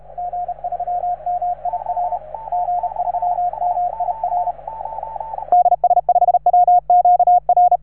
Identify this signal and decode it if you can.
Morse code, heard as a keyed tone, about 35 wpm, 710 Hz, with radio noise. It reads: G3MIR G3MIR DS5WQR
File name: RS3A-20